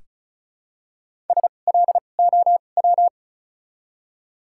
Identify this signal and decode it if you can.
Morse code, heard as a keyed tone, about 36 wpm, 700 Hz. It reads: S L O W
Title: 2x Character Spacing - Top 500 Words - 35wpm